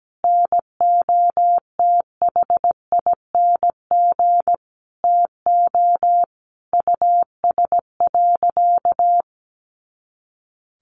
Morse code, heard as a keyed tone, about 17 wpm, 700 Hz. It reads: NOTHING TO US.
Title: Sentences from Top 500 Words 17wpm